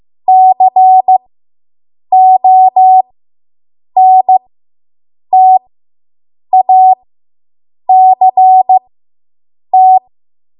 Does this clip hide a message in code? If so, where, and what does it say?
Yes: Morse code from 0.3 to 10.0 s, heard as a keyed tone, about 15 wpm, spaced as for 7 wpm, 750 Hz. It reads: CONTACT